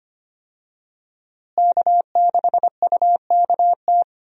Morse code, heard as a keyed tone, about 25 wpm, 700 Hz. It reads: K6UKT
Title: USA Call Signs - 5 Characters - Repeated 3x - 25wpm